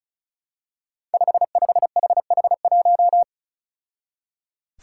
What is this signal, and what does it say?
Morse code, heard as a keyed tone, about 34 wpm, 700 Hz. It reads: 55HH1
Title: Five Challenging Letter-Number Combinations 35wpm